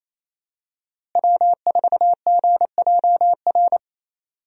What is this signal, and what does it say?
Morse code, heard as a keyed tone, about 28 wpm, 700 Hz. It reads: W4GJR
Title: USA Call Signs - 5 Characters - Repeated 3x - 28wpm